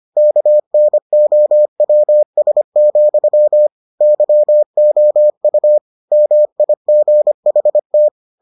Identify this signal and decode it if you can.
Morse code, heard as a keyed tone, about 25 wpm, 600 Hz. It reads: KNOWS, YOU MIGHT